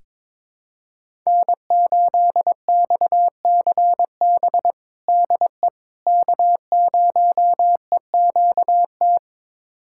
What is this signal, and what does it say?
Morse code, heard as a keyed tone, about 22 wpm, 700 Hz. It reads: N8XCB DE K0EQT
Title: USA Callsign DE Callsign - 5 Characters - 22wpm